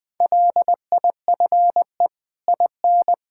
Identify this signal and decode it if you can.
Morse code, heard as a keyed tone, about 21 wpm, 700 Hz. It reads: LIFE IN
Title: Sets of 8 Words - 20wpm